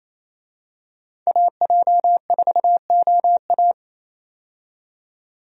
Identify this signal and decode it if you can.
Morse code, heard as a keyed tone, about 28 wpm, 700 Hz. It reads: AJ4OA
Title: USA Call Signs - 5 Characters - 28wpm